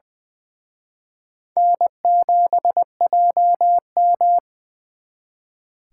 Morse code, heard as a keyed tone, about 24 wpm, 700 Hz. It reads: N7JM
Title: Speed Racing - USA 1x2 Call Signs 20wpm